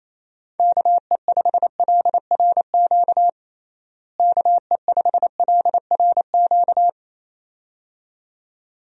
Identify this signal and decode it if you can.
Morse code, heard as a keyed tone, about 28 wpm, 700 Hz. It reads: KE5LRQ KE5LRQ
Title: USA Call Signs - 6 Characters - Repeated 2x - 28wpm